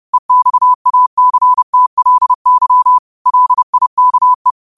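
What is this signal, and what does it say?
Morse code, heard as a keyed tone, about 30 wpm, 1000 Hz. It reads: EXACTLY LIKE
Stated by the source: Morse code quotes of the day (QOTD) sent at 30 words per minute.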